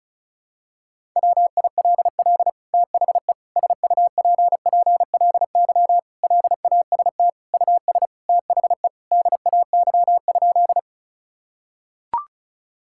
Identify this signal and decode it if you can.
Morse code, heard as a keyed tone, about 36 wpm, 700 Hz. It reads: WILL THE SUPPLY LAST US THE DAY?
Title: Sentences from Top 1000 Words - Review Part 5 - 35wpm